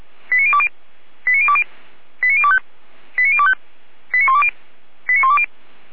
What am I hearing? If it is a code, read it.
Five-tone selective call (ZVEI), heard as quick runs of five tones, tones of about 70 ms. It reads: 89020 89020 89025 89025 89120 89120
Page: ZVEI